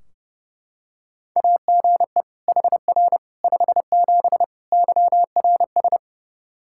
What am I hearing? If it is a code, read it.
Morse code, heard as a keyed tone, about 31 wpm, 700 Hz. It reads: AGE HR 57 YRS
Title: Practice QSO - Exchange 3 - More Info - 30wpm